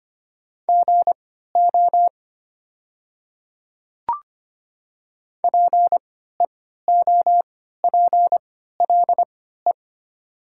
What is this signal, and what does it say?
Morse code, heard as a keyed tone, about 25 wpm, spaced as for 14 wpm, 690 Hz. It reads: GO PEOPLE
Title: 3x Character Spacing - Top 100 Words - 25wpm